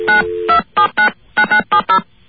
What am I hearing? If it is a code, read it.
DTMF, heard as telephone touch tones, tones of about 100 ms. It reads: 92799970